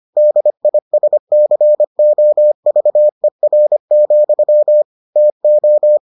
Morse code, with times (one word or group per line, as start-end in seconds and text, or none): 0.2-4.8 DISCOVER,
5.2-6.0 TO